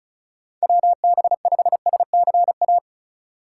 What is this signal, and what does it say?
Morse code, heard as a keyed tone, about 34 wpm, 700 Hz. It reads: WB5SCA
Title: USA Call Signs - 6 Characters - 4x Character Spacing - 35wpm